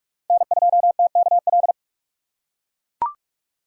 Morse code, heard as a keyed tone, about 43 wpm, 700 Hz. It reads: N2TKL
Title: USA Call Signs - 5 Characters - 3x Character Spacing - 45wpm